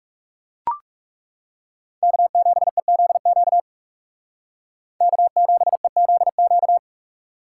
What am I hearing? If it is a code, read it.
Morse code, heard as a keyed tone, about 38 wpm, 700 Hz. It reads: K7EZQ K7EZQ